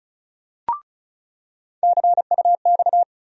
Morse code, heard as a keyed tone, about 36 wpm, 700 Hz. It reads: CUX
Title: USA Call Sign Postfixes - 3 Letters 35wpm